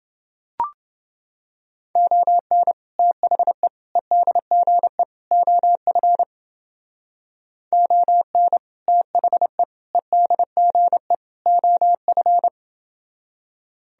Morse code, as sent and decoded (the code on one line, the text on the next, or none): --- -. / - .... / . -.. --. / --- ..-. / --- -. / - .... / . -.. --. / --- ..-.
ON THE EDGE OF ON THE EDGE OF